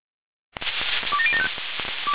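Five-tone selective call (ZVEI), heard as quick runs of five tones, tones of about 70 ms. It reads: 27075
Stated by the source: ZVEI five-tone call received by SDR.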